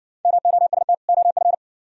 Morse code, heard as a keyed tone, about 55 wpm, 700 Hz. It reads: MOST OF